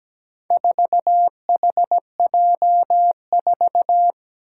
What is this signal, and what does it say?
Morse code, heard as a keyed tone, about 17 wpm, 700 Hz. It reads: 4HJ4
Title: Four Challenging Letter-Number Combinations 17wpm